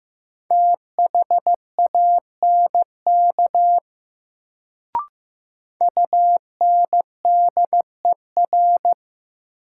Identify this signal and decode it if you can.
Morse code, heard as a keyed tone, about 15 wpm, 700 Hz. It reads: THANK UNDER
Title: Sets of 2 Words - New Word Drill - 15wpm